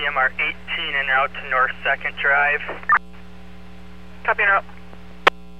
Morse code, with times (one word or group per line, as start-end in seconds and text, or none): none